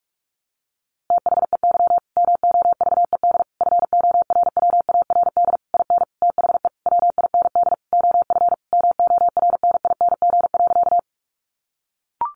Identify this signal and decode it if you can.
Morse code, heard as a keyed tone, about 44 wpm, 700 Hz. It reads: THEY MOVED FORWARD IN THE WIND OF MORNING.
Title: A Sound Of Thunder 45wpm